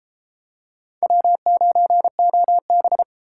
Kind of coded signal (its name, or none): Morse code